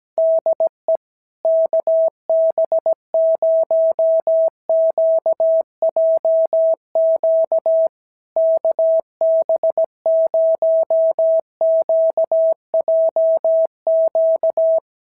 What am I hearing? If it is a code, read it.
Morse code, heard as a keyed tone, about 17 wpm, 660 Hz. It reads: DE KB0QJQ KB0QJQ
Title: SOTA QSO Practice 17wpm